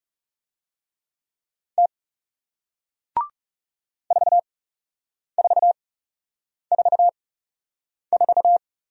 Morse code, heard as a keyed tone, about 38 wpm, spaced as for 8 wpm, 700 Hz. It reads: T 4444